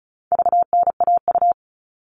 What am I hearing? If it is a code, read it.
Morse code, heard as a keyed tone, about 34 wpm, 700 Hz. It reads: VNAU
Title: All Cut Numbers - 4-Digit 35wpm